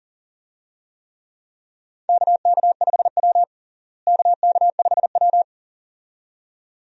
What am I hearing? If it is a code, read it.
Morse code, heard as a keyed tone, about 40 wpm, 700 Hz. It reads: KK5W KK5W
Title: USA Call Signs - 4 Characters - Repeated 2x - 40wpm